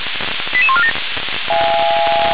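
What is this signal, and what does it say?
Five-tone selective call (ZVEI), heard as quick runs of five tones, tones of about 70 ms. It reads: 88147
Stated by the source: Five-tone ZVEI selective call received by SDR.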